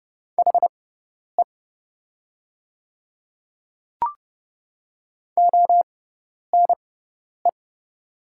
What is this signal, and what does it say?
Morse code, heard as a keyed tone, about 31 wpm, spaced as for 10 wpm, 700 Hz. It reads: HE ONE
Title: 6x Character Spacing - Top 100 Words - 30wpm